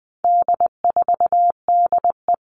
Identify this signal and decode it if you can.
Morse code, heard as a keyed tone, about 20 wpm, 700 Hz. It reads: D4DE
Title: All Cut Numbers - 4-Digit 20wpm